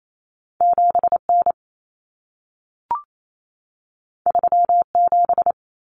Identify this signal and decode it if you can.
Morse code, heard as a keyed tone, about 28 wpm, 700 Hz. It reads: 7N 37
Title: Common Cut Numbers - 2-Digit 28wpm